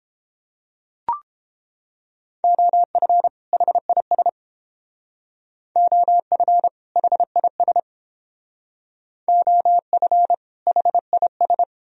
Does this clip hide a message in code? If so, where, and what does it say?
Yes: Morse code from 2.4 to 11.6 s, heard as a keyed tone, about 30 wpm, 700 Hz. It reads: OF HIS OF HIS OF HIS